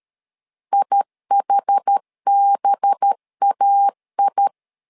Morse code, heard as a keyed tone, about 13 wpm, 800 Hz. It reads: IHBAI
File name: LEZIONI / 5 LBVC / 5 LB 12wpm 1906